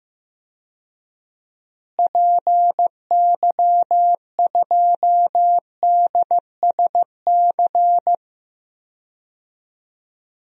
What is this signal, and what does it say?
Morse code, heard as a keyed tone, about 15 wpm, 700 Hz. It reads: PY2DSC
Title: DXCC-100 Call Signs - 6 Characters - Repeated 2x - 15wpm